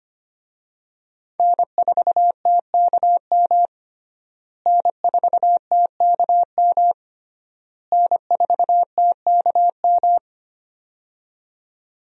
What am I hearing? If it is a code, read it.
Morse code, heard as a keyed tone, about 25 wpm, 700 Hz. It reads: N4TKM N4TKM N4TKM
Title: USA Call Signs - 5 Characters - Repeated 3x - 25wpm